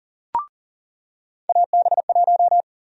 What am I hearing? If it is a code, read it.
Morse code, heard as a keyed tone, about 40 wpm, 700 Hz. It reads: AB1
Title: USA Call Sign Prefixes - 40wpm